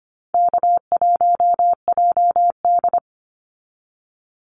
Morse code, heard as a keyed tone, about 25 wpm, 700 Hz. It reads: K1JD
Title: NA SOTA Activator Callsigns 25wpm